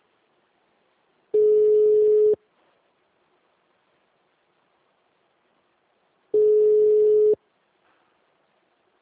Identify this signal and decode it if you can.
Ringback tone (Europe, 425 Hz).
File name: out- -2004- - - .1684